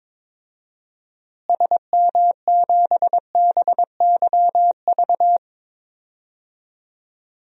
Morse code, heard as a keyed tone, about 22 wpm, 700 Hz. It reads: SM7BYV
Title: DXCC-100 Call Signs - 6 Characters - Repeated 3x - 22wpm